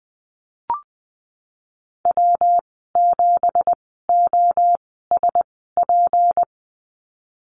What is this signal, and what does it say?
Morse code, heard as a keyed tone, about 20 wpm, spaced as for 14 wpm, 700 Hz. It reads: W7OSP
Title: 231 Lesson 37g - Letter x - Call Signs 20wpm